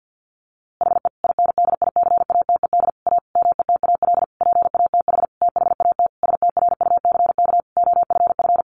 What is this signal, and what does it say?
Morse code, heard as a keyed tone, about 50 wpm, 700 Hz. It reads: HE INDICATED A METAL PATH THAT STRUCK OFF
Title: A Sound Of Thunder 50wpm